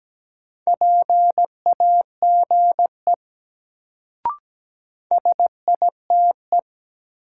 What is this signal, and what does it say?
Morse code, heard as a keyed tone, about 17 wpm, 700 Hz. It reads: PAGE SITE
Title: Sentences from Top 1000 Words - Review Part 1 - New Word Drill - 17wpm